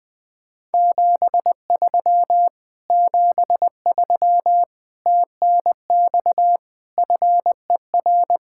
Morse code, heard as a keyed tone, about 20 wpm, 700 Hz. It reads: 73 73 TNX FER